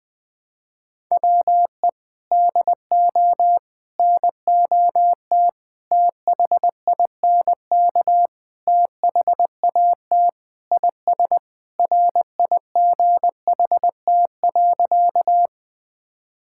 Morse code, heard as a keyed tone, about 20 wpm, 700 Hz. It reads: WE DO NOT THINK THAT IS RIGHT.
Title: Sentences with 4-Word Phrases - 20wpm